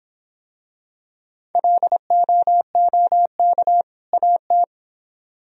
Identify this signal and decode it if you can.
Morse code, heard as a keyed tone, about 25 wpm, 700 Hz. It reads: LOOK AT